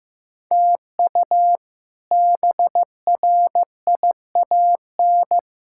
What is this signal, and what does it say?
Morse code, heard as a keyed tone, about 15 wpm, 700 Hz. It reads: TU BRIAN
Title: K1USN SST QSO Practice - 15wpm